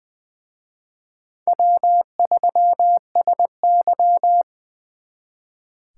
Morse code, heard as a keyed tone, about 24 wpm, 700 Hz. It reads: W3SY
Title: Speed Racing - USA 1x2 Call Signs 20wpm